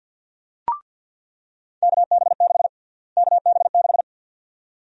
Morse code, heard as a keyed tone, about 48 wpm, 700 Hz. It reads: KB6 KB6
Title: USA Call Sign Prefixes - Repeated 2x - 50wpm